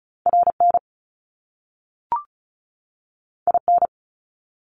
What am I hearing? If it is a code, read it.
Morse code, heard as a keyed tone, about 36 wpm, 700 Hz. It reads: RN IN
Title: Top Two Letter Combinations in Words 35wpm